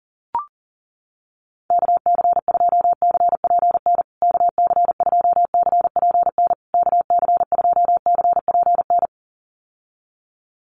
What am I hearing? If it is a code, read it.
Morse code, heard as a keyed tone, about 40 wpm, 700 Hz. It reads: KC2CPN KC2CPN KC2CPN
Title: USA Call Signs Repeated 3x 40wpm